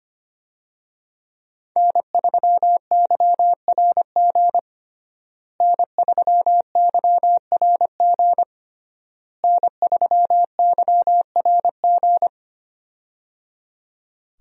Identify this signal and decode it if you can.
Morse code, heard as a keyed tone, about 25 wpm, 700 Hz. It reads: N3YRG N3YRG N3YRG
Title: USA Call Signs - 5 Characters - Repeated 3x - 25wpm